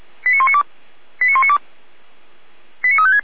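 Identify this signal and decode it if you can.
Five-tone selective call (ZVEI), heard as quick runs of five tones, tones of about 70 ms. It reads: 89192 89192 89356
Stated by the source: ZVEI five-tone selective call received on an SDR.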